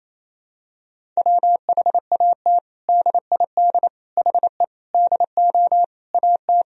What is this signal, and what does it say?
Morse code, heard as a keyed tone, about 28 wpm, 700 Hz. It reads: WHAT DID HE DO AT
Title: Sentences from Top 1000 Words - Review Part 1 - 28wpm